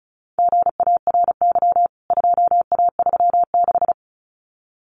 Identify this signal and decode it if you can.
Morse code, heard as a keyed tone, about 34 wpm, 700 Hz. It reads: GARY 2A36
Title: CWT Exchange 35wpm